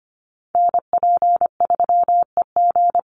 Morse code, heard as a keyed tone, about 25 wpm, 700 Hz. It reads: NP3EG
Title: USA Call Signs 25wpm